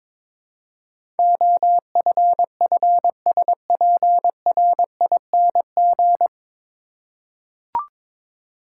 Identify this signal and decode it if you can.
Morse code, heard as a keyed tone, about 22 wpm, 700 Hz. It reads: OFFSPRING
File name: 9 Letter Words - First 3 Letters Given - 22wpm